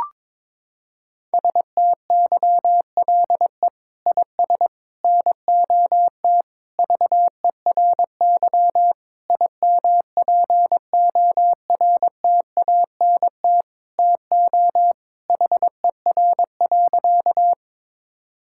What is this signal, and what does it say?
Morse code, heard as a keyed tone, about 22 wpm, 700 Hz. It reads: STYLE IS NOT VERY IMPORTANT TO HER.
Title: Sentences from Top 1000 Words - Review Part 5 - 22wpm